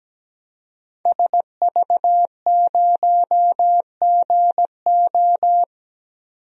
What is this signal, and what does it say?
Morse code, heard as a keyed tone, about 17 wpm, 700 Hz. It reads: SV0GO
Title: DXCC-100 Call Signs - 5 Characters - 17wpm